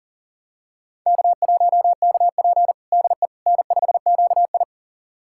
Morse code, heard as a keyed tone, about 40 wpm, 700 Hz. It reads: K1KP DE N5QI
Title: USA Callsign DE Callsign - 4 Characters - 40wpm